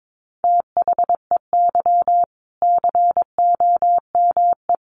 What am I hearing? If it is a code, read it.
Morse code, heard as a keyed tone, about 22 wpm, 700 Hz. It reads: THEY COME
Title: Sentences from Top 1400 Words 22wpm